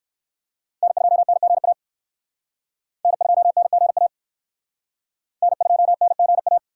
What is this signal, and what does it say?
Morse code, heard as a keyed tone, about 56 wpm, 700 Hz. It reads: N2NGA N2NGA N2NGA
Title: Speed Racing - USA 1x3 Call Signs 45wpm